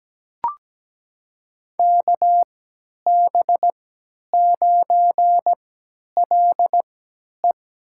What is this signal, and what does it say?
Morse code, heard as a keyed tone, about 17 wpm, spaced as for 10 wpm, 700 Hz. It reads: KB9LE
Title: USA Call Signs - 3x Character Spacing - 17wpm